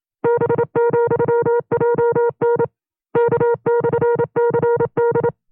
Morse code, heard as a keyed tone, about 27 wpm, 470 Hz. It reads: B,JN K/CD
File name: 25WPM